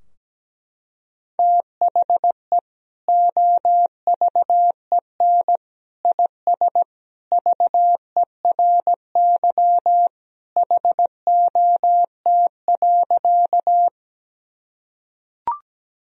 Morse code, heard as a keyed tone, about 17 wpm, 700 Hz. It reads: THE OVEN IS VERY HOT.
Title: Sentences from Top 1000 Words - Review Part 4 - 17wpm